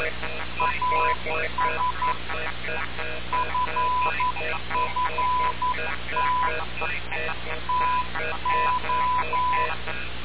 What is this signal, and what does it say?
Morse code, heard as a keyed tone, about 11 wpm, 1000 Hz. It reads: AS FFT TO
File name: radiocap / REC z